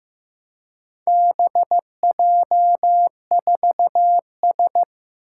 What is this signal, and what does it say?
Morse code, heard as a keyed tone, about 15 wpm, 700 Hz. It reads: BJ4S
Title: Four Challenging Letter-Number Combinations 15wpm